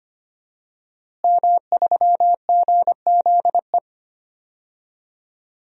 Morse code, heard as a keyed tone, about 25 wpm, 700 Hz. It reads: M3GZE
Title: DXCC-100 Call Signs - 5 Characters - Repeated 2x - 25wpm